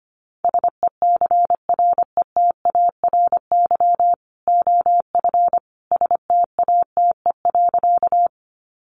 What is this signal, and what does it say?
Morse code, heard as a keyed tone, about 25 wpm, 700 Hz. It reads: SECRETARY OF STATE.
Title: Sentences from Top 1300 Words 25wpm